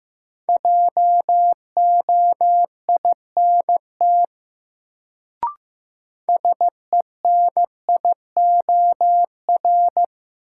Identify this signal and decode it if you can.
Morse code, heard as a keyed tone, about 15 wpm, 700 Hz. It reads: JOINT SENIOR
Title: Sets of 5 Words Encore - New Word Drill - 15wpm